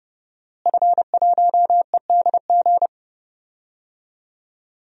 Morse code, heard as a keyed tone, about 31 wpm, 700 Hz. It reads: F1EDG
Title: DXCC-100 Call Signs - 5 Characters - Repeated 2x - 30wpm